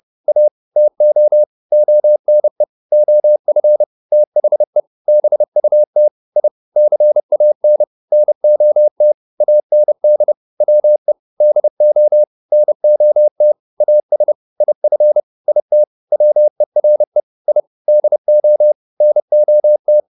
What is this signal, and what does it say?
Morse code, heard as a keyed tone, about 30 wpm, 600 Hz. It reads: A TO ONE OF THE BUT I CAN NOT AND WE DO NOT AS IF IT WERE I DO NOT